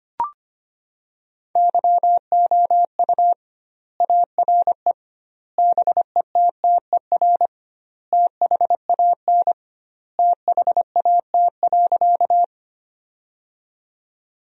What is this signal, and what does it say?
Morse code, heard as a keyed tone, about 25 wpm, 700 Hz. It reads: YOU ARE BETTER THAN THAT.
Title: 2x Word-Spacing - Sentences from Top 400 Words - 25wpm